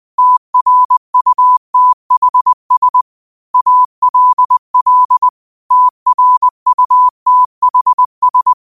Morse code, heard as a keyed tone, about 20 wpm, 1000 Hz. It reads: TRUTHS ALL TRUTHS
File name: qotdfiles / 20wpm